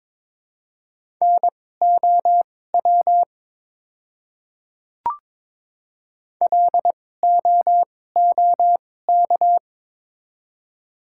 Morse code, heard as a keyed tone, about 22 wpm, spaced as for 16 wpm, 690 Hz. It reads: NOW LOOK
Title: 2x Character Spacing - Top 100 Words - 22wpm